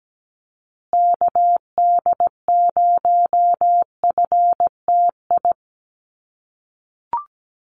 Morse code, heard as a keyed tone, about 17 wpm, 700 Hz. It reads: KD0FTI
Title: USA Call Signs Repeated 3x 17wpm